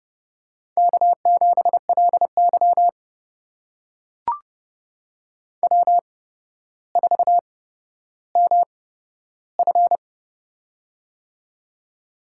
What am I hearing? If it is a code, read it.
Morse code, heard as a keyed tone, about 30 wpm, spaced as for 13 wpm, 700 Hz. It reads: K7LY W4MF
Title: USA Call Signs - 4 Characters - 8x Character Spacing - 30wpm